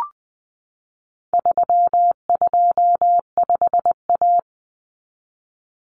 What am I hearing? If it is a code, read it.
Morse code, heard as a keyed tone, about 20 wpm, 700 Hz. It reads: 325A
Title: Common Cut Numbers - 4-Digit 20wpm